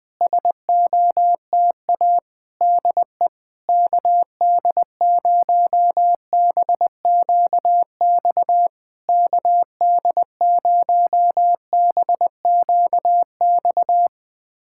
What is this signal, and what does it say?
Morse code, heard as a keyed tone, about 20 wpm, 700 Hz. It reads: SOTA DE KD0BQX KD0BQX
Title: SOTA QSO Practice 20wpm